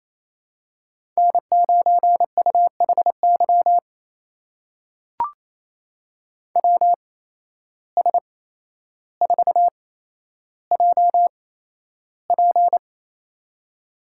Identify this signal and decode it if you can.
Morse code, heard as a keyed tone, about 28 wpm, spaced as for 12 wpm, 700 Hz. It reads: N9UHY WS4JP
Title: USA Call Signs - 5 Characters - 8x Character Spacing - 28wpm